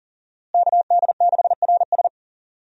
Morse code, heard as a keyed tone, about 40 wpm, 700 Hz. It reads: KD6RS
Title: USA Call Signs - 5 Characters - Repeated 3x - 40wpm